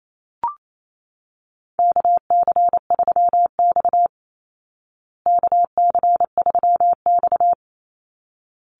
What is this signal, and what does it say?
Morse code, heard as a keyed tone, about 28 wpm, 690 Hz. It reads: KC3X KC3X
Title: Call Signs Repeated 3x - 28wpm